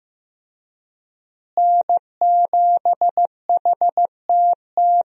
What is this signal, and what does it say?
Morse code, heard as a keyed tone, about 15 wpm, 700 Hz. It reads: N7HTT
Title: USA Call Signs - 5 Characters - Repeated 2x - 15wpm